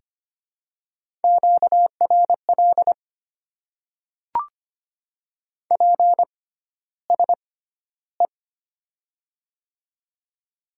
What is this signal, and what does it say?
Morse code, heard as a keyed tone, about 25 wpm, spaced as for 13 wpm, 700 Hz. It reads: QRL PSE
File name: Top 90 QSO Elements - 6x Character Spacing - 25wpm